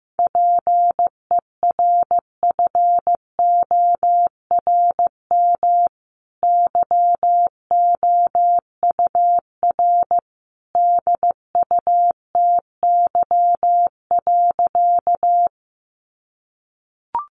Morse code, heard as a keyed tone, about 15 wpm, 700 Hz. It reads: PERFORM YOUR DUTY.
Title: Sentences from Top 1500 Words 15wpm